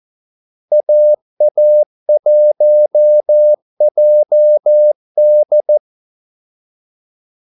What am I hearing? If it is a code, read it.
Morse code, heard as a keyed tone, about 14 wpm, 600 Hz. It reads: AA1JD
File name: Callsigns-14wpm